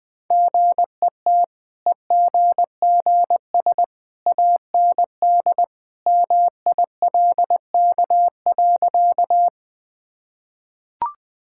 Morse code, heard as a keyed tone, about 20 wpm, 700 Hz. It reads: GET EGGS AND MILK.
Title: Easy Sentences with Hints - 20wpm